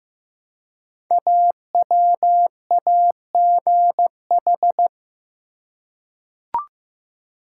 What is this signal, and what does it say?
Morse code, heard as a keyed tone, about 15 wpm, 700 Hz. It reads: AWAGH
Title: Code Groups - 5 Letter:Numbers - 15wpm